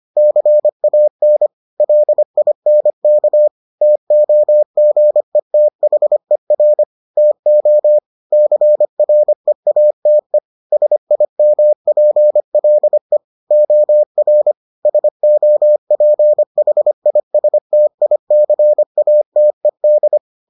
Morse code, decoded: CAN LINK TOGETHER TO CREATE SIMPLE OR SOPHISTICATED

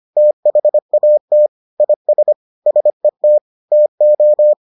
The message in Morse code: THAT IS SET TO

